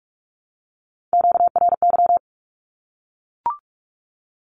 Morse code, heard as a keyed tone, about 44 wpm, 700 Hz. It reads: QRY
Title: Q-Codes 45wpm